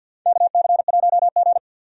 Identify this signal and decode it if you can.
Morse code, heard as a keyed tone, about 52 wpm, 700 Hz. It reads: KC1G
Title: USA Call Signs - 4 Characters - Repeated 3x - 50wpm